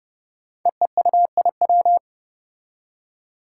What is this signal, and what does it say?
Morse code, heard as a keyed tone, about 30 wpm, 700 Hz. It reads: EEUIW
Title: Five Challenging Letter-Number Combinations 30wpm